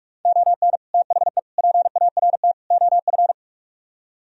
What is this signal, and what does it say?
Morse code, heard as a keyed tone, about 48 wpm, 700 Hz. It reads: ON THE PART OF